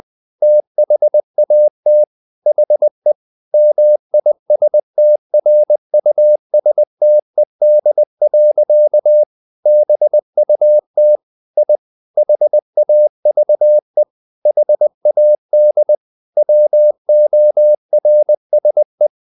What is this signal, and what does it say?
Morse code, heard as a keyed tone, about 20 wpm, 600 Hz. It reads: THAT HE MISTRUSTED. BUT I HAVE HAD WORSE
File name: the old man and the sea part 2 20wpm